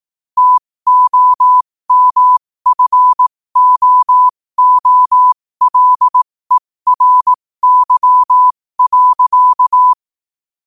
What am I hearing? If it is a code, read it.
Morse code, heard as a keyed tone, about 18 wpm, spaced as for 15 wpm, 1000 Hz. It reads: TOMFOOLERY.
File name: qotdfiles / 15wpm